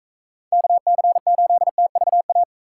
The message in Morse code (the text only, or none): KC8TVA